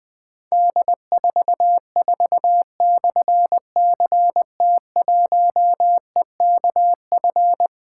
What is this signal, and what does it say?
Morse code, heard as a keyed tone, about 20 wpm, 700 Hz. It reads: D44/CT1EKF
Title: Difficult Call Signs 20wpm